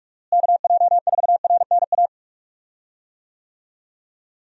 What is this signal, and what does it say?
Morse code, heard as a keyed tone, about 46 wpm, 700 Hz. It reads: KJ4RNA
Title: USA Call Signs - 6 Characters - Repeated 2x - 45wpm